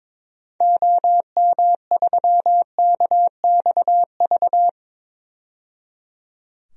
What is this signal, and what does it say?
Morse code, heard as a keyed tone, about 22 wpm, 700 Hz. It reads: OM3KXV
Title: DXCC-100 Call Signs - 6 Characters - 22wpm